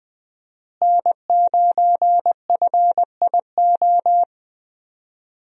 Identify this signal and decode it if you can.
Morse code, heard as a keyed tone, about 20 wpm, 700 Hz. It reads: N9FIO
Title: USA Call Signs - 5 Characters - Repeated 3x - 20wpm